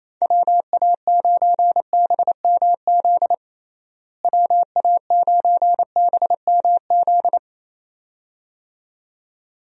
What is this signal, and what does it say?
Morse code, heard as a keyed tone, about 27 wpm, 700 Hz. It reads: WA9BMZ WA9BMZ
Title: USA Call Signs - 6 Characters - Repeated 2x - 28wpm